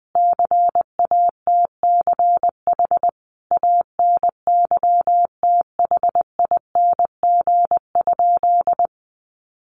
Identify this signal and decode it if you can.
Morse code, heard as a keyed tone, about 20 wpm, 700 Hz. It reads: CATCH ANYTHING?
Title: Sentences from Top 1500 Words 20wpm